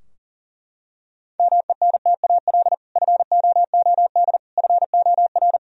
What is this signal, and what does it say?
Morse code, heard as a keyed tone, about 40 wpm, 700 Hz. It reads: MENTAL FOOD FOR
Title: Sentences from Top 1000 Words - Review Part 4 - 40wpm